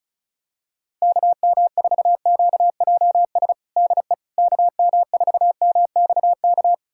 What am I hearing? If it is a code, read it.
Morse code, heard as a keyed tone, about 34 wpm, 700 Hz. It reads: KM4QJS DE KM4MXK
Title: USA Callsign DE Callsign - 6 Characters - 35wpm